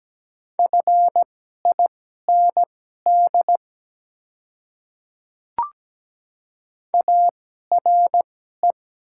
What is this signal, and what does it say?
Morse code, heard as a keyed tone, about 17 wpm, spaced as for 12 wpm, 700 Hz. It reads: FIND ARE
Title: 2x Character Spacing - Top 100 Words - 17wpm